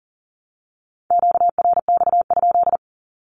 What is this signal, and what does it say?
Morse code, heard as a keyed tone, about 40 wpm, 700 Hz. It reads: QRX?
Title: Q-Codes Repeated 3x 40wpm